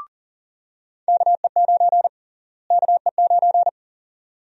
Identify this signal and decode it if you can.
Morse code, heard as a keyed tone, about 40 wpm, 700 Hz. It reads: KE9 KE9
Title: USA Call Sign Prefixes - Repeated 2x - 40wpm